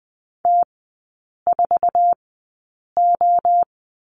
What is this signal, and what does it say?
Morse code, heard as a keyed tone, about 20 wpm, spaced as for 8 wpm, 700 Hz. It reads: T4O
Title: 047 Lesson 9e - Number 4 - Three Characters 20wpm